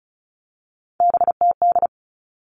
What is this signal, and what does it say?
Morse code, heard as a keyed tone, about 34 wpm, 700 Hz. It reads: BTD
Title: All Cut Numbers - 3-Digit 35wpm